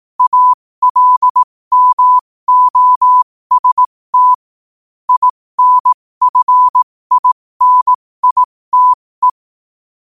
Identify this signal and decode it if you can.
Morse code, heard as a keyed tone, about 18 wpm, spaced as for 15 wpm, 1000 Hz. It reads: ALMOST INFINITE
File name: qotdfiles / 15wpm